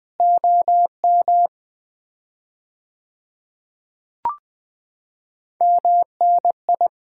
Morse code, heard as a keyed tone, about 20 wpm, 700 Hz. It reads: OM MNI
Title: Common QSO Phrases - 4x Word-Spacing - 20wpm